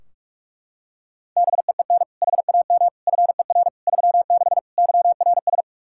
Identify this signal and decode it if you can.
Morse code, heard as a keyed tone, about 42 wpm, 700 Hz. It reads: BEEN HAM FER 36 YRS
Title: Practice QSO - Exchange 3 - More Info - 45wpm